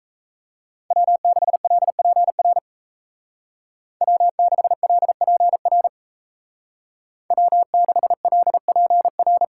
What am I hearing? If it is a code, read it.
Morse code, heard as a keyed tone, about 35 wpm, 700 Hz. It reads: W6LPR W6LPR W6LPR
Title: Speed Racing - USA 1x3 Call Signs 28wpm